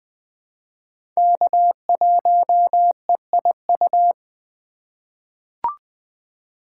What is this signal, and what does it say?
Morse code, heard as a keyed tone, about 20 wpm, 700 Hz. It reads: K1EIU
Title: USA Call Signs - 5 Characters - 2x Character Spacing - 20wpm